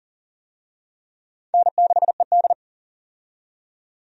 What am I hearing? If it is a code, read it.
Morse code, heard as a keyed tone, about 40 wpm, 700 Hz. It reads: N6ED
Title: USA Call Signs - 4 Characters - 40wpm